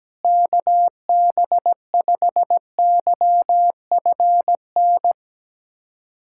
Morse code, heard as a keyed tone, about 17 wpm, 700 Hz. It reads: KB5YFN
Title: SOTA QSO Practice 17wpm